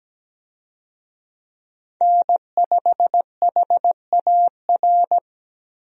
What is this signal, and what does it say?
Morse code, heard as a keyed tone, about 17 wpm, 700 Hz. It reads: N5HAR
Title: USA Call Signs - 5 Characters - Repeated 3x - 17wpm